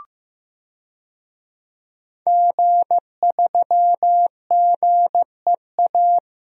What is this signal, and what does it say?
Morse code, heard as a keyed tone, about 15 wpm, 700 Hz. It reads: G3GEA
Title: DXCC-100 Call Signs - 5 Characters - Repeated 3x - 15wpm